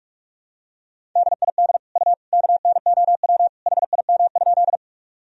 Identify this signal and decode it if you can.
Morse code, heard as a keyed tone, about 46 wpm, 700 Hz. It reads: DID U KNOW HIM?
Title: Sentences from Top 100 Words and Abbreviations 45wpm